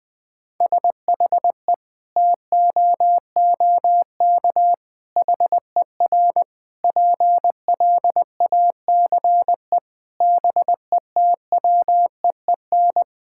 Morse code, heard as a keyed tone, about 20 wpm, 700 Hz. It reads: SHE TOOK HER PLACE BETWEEN
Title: Sentences from Top 1000 Words - Review Part 5 - 20wpm